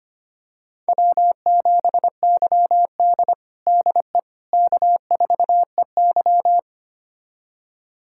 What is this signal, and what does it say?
Morse code, heard as a keyed tone, about 25 wpm, 700 Hz. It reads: W7YD DE K4EY
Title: USA Callsign DE Callsign - 4 Characters - 25wpm